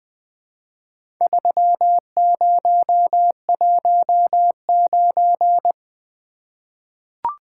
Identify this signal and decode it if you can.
Morse code, heard as a keyed tone, about 20 wpm, 700 Hz. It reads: 3019
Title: Four Letter-Number Combinations - 20wpm